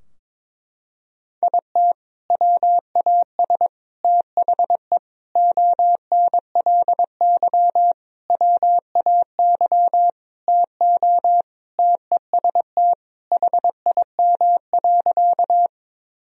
Morse code, heard as a keyed tone, about 22 wpm, 700 Hz. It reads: IT WAS THE ONLY WAY TO TEST HIM.
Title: Sentences from Top 1000 Words - Review Part 3 - 22wpm